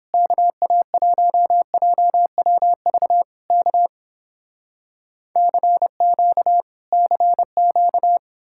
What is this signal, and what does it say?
Morse code, heard as a keyed tone, about 29 wpm, 700 Hz. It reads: KA1JWV K CQ CQ